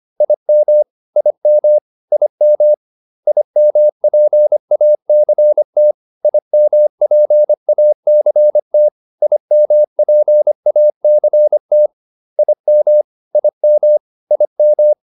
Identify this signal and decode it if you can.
Morse code, heard as a keyed tone, about 25 wpm, 600 Hz. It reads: IM IM IM IMPACT IMPACT IMPACT IM IM IM